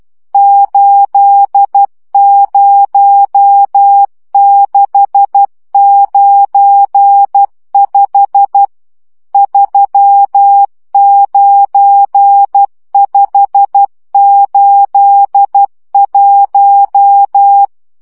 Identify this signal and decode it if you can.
Morse code, heard as a keyed tone, about 12 wpm, 800 Hz. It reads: 80695 39581
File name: morze vizsga 2011-2 10